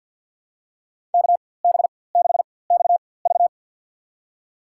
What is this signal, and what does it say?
Morse code, heard as a keyed tone, about 51 wpm, spaced as for 23 wpm, 700 Hz. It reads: KB6XV
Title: USA Call Signs - 5 Characters - 4x Character Spacing - 50wpm